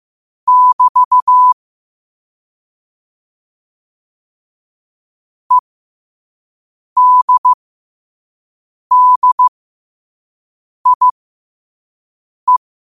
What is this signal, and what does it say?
Morse code, heard as a keyed tone, about 15 wpm, spaced as for 5 wpm, 1000 Hz. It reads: <BT> EDDIE
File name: qotdfiles / 5wpm